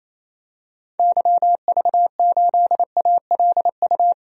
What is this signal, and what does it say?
Morse code, heard as a keyed tone, about 28 wpm, 700 Hz. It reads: YV8ALU
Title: DXCC-100 Call Signs - 6 Characters - 28wpm